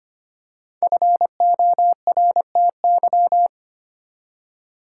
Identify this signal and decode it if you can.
Morse code, heard as a keyed tone, about 25 wpm, 700 Hz. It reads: FORTY
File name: 5 Letter Words - First 3 Letters Given - 25wpm